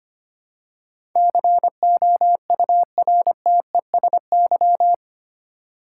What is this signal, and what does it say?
Morse code, heard as a keyed tone, about 25 wpm, 700 Hz. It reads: COURTESY
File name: Eight Letter Words - 25wpm